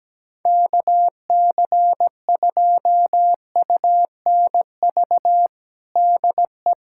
Morse code, heard as a keyed tone, about 17 wpm, 700 Hz. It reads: KC2UNV DE
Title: USA Callsign DE Callsign - 6 Characters - 17wpm